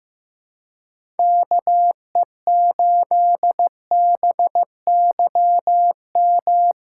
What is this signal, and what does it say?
Morse code, heard as a keyed tone, about 15 wpm, 700 Hz. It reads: KE8BYM
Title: USA Callsign DE Callsign - 6 Characters - 15wpm